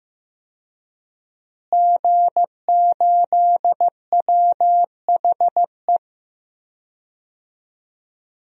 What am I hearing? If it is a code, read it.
Morse code, heard as a keyed tone, about 16 wpm, 700 Hz. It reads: G8WHE
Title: DXCC-100 Call Signs - 5 Characters - Repeated 2x - 15wpm